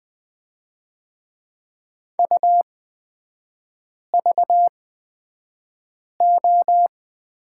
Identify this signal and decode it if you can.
Morse code, heard as a keyed tone, about 20 wpm, spaced as for 5 wpm, 690 Hz. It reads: UVO
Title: 255 Lesson 41e - Period - Three Characters 20wpm